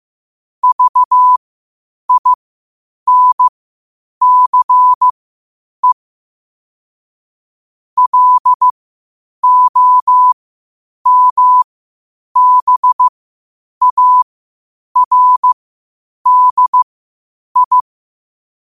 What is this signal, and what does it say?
Morse code, heard as a keyed tone, about 15 wpm, spaced as for 8 wpm, 1000 Hz. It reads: VINCE LOMBARDI
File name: qotdfiles / 8wpm